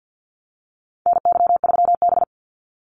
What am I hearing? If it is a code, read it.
Morse code, heard as a keyed tone, about 50 wpm, 700 Hz. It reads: NY3B
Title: Call Signs Repeated 3x - 50wpm